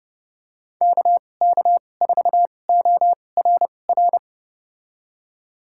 Morse code, heard as a keyed tone, about 30 wpm, spaced as for 22 wpm, 700 Hz. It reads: KK4ORR
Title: USA Call Signs - 2x Character Spacing - 30wpm